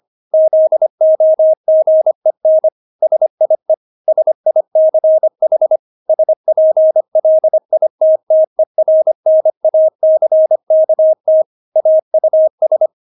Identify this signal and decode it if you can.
Morse code, heard as a keyed tone, about 25 wpm, 630 Hz. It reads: ZOGEN SIE SICH SPLITTERNACKT AUS
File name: Witz6 25wpm